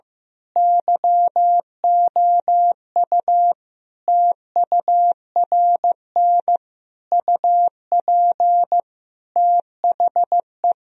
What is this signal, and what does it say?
Morse code, heard as a keyed tone, about 16 wpm, 700 Hz. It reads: YOU TURN UP THE
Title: Sentences from Top 1000 Words - Review Part 3 - 15wpm